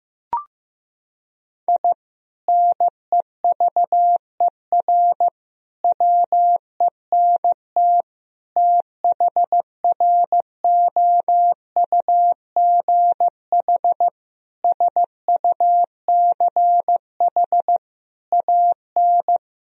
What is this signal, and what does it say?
Morse code, heard as a keyed tone, about 14 wpm, 700 Hz. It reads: I NEVER WENT THROUGH SUCH AN
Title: Sentences from Top 1000 Words - Review Part 3 - 15wpm